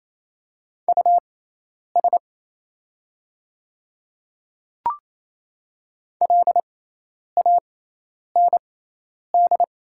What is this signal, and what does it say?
Morse code, heard as a keyed tone, about 28 wpm, spaced as for 10 wpm, 700 Hz. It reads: US LAND
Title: 6x Character Spacing - Top 300 Words - 28wpm